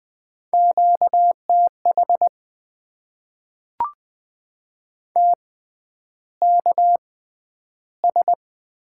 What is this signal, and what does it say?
Morse code, heard as a keyed tone, about 20 wpm, spaced as for 10 wpm, 700 Hz. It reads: QTH TKS
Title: Top 90 QSO Elements - 6x Character Spacing - 20wpm